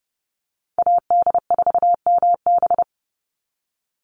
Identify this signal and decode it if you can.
Morse code, heard as a keyed tone, about 30 wpm, 700 Hz. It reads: AD4MB
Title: USA Call Signs 30wpm